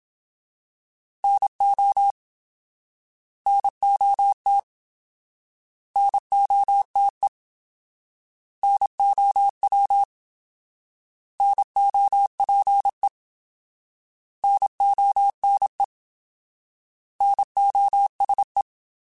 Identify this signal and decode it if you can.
Morse code, heard as a keyed tone, about 27 wpm, 800 Hz. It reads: NO NOT NOTE NOW NOPE NONE NOSE
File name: Alphabetical / Alphabetical mp3s / Ns @25